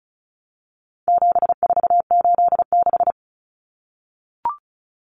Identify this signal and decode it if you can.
Morse code, heard as a keyed tone, about 34 wpm, 700 Hz. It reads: 7486
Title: Common Cut Numbers - 4-Digit 35wpm